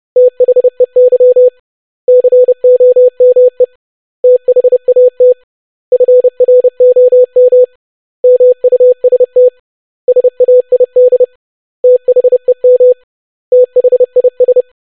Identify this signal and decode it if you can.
Morse code, heard as a keyed tone, about 30 wpm, 500 Hz. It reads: THEY COME THAT FROM MUST SAID THEM THIS
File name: basic100-25wpm